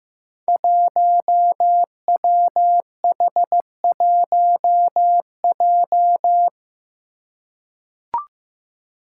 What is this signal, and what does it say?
Morse code, heard as a keyed tone, about 15 wpm, 700 Hz. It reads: 1WH1J
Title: Five Challenging Letter-Number Combinations 15wpm